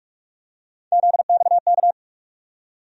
Morse code, heard as a keyed tone, about 43 wpm, 700 Hz. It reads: ZXK
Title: USA Call Sign Postfixes - 3 Letters 45wpm